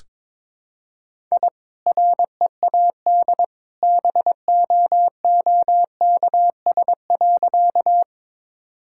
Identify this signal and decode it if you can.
Morse code, heard as a keyed tone, about 22 wpm, 700 Hz. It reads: I READ BOOKS.
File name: practice / 22wpm / Easy Sentences with Hints - 22wpm